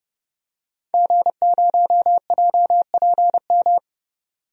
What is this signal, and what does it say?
Morse code, heard as a keyed tone, about 31 wpm, 700 Hz. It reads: G0JPM
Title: DXCC-100 Call Signs - 5 Characters - 30wpm